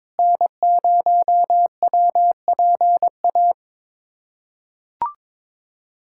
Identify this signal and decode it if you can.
Morse code, heard as a keyed tone, about 21 wpm, 700 Hz. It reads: N0WPA
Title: USA Callsign DE Callsign 22wpm